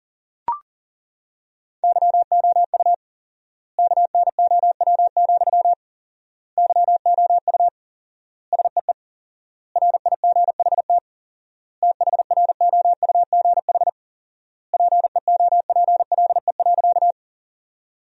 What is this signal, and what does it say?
Morse code, heard as a keyed tone, about 40 wpm, 700 Hz. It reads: YOU KNOW, YOU SEE RIGHT THROUGH PEOPLE.
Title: 4x Word-Spacing - Sentences from Top 200 Words - 40wpm